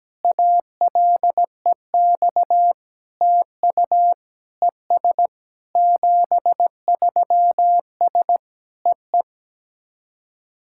Morse code, heard as a keyed tone, about 17 wpm, 700 Hz. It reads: ALEX TU ES 73S EE
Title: SOTA QSO Practice 17wpm